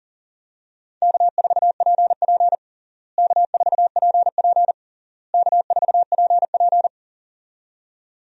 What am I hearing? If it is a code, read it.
Morse code, heard as a keyed tone, about 40 wpm, 700 Hz. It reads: K4PP K4PP K4PP
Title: USA Call Signs - 4 Characters - Repeated 3x - 40wpm